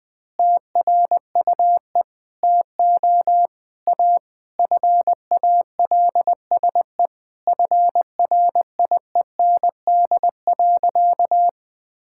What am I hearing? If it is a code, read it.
Morse code, heard as a keyed tone, about 21 wpm, 700 Hz. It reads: TRUE TO A FALSE FRIEND.
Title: Sentences from Top 1600 Words 20wpm